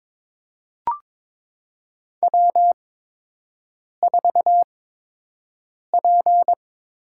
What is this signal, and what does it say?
Morse code, heard as a keyed tone, about 22 wpm, spaced as for 6 wpm, 700 Hz. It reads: W4P
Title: USA Call Signs - 8x Character Spacing - 22wpm